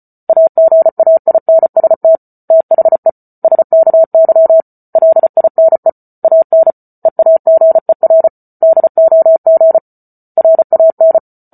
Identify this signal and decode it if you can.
Morse code, heard as a keyed tone, about 34 wpm, 650 Hz. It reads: AGAINST THE SKY LINE AN EAGER DOG RAN
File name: Book Two / 08-Chapter 8 - 34 WPM